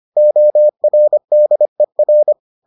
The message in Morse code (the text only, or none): ORDER